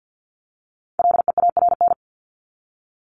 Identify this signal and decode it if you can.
Morse code, heard as a keyed tone, about 50 wpm, 700 Hz. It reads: LEARN
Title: ING Suffix - Top 100 Words with Base Words 50wpm